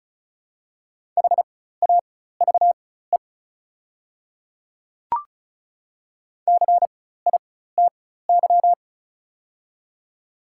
Morse code, heard as a keyed tone, about 34 wpm, spaced as for 16 wpm, 700 Hz. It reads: HAVE CITY
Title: 4x Character Spacing - Top 300 Words - 35wpm